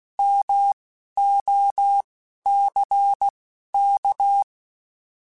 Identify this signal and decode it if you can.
Morse code, heard as a keyed tone, about 16 wpm, spaced as for 11 wpm, 800 Hz. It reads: MOCK
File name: Alphabetical / Alphabetical mp3s / Ms @10